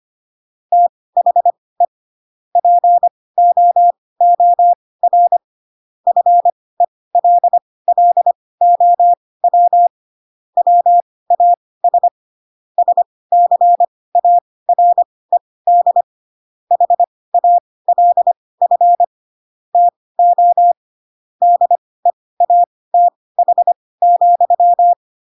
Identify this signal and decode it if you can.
Morse code, heard as a keyed tone, about 25 wpm, spaced as for 18 wpm, 700 Hz. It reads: THE POOR FELLOW WAS SCARED HALF TO DEATH,
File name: Pinocchio / Pinocchio ch1 12 25 ik1hge